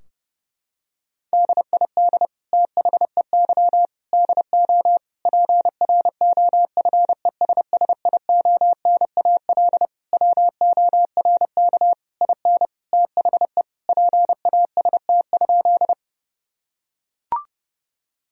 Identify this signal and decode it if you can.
Morse code, heard as a keyed tone, about 30 wpm, 700 Hz. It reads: DID THEY DO PROFESSIONAL WORK IN THE PAST?
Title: Sentences from Top 1000 Words - Review Part 4 - 30wpm